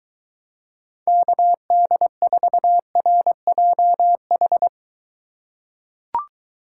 Morse code, heard as a keyed tone, about 23 wpm, 700 Hz. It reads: KD4RJH